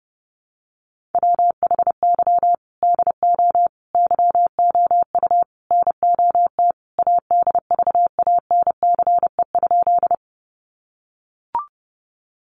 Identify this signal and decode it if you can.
Morse code, heard as a keyed tone, about 30 wpm, 700 Hz. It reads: WHY DO YOU NOT ADVANCE?
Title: Sentences from Top 1400 Words 30wpm